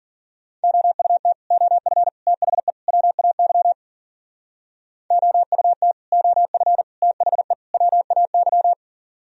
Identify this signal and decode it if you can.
Morse code, heard as a keyed tone, about 52 wpm, 700 Hz. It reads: OUT OF THE WAY OUT OF THE WAY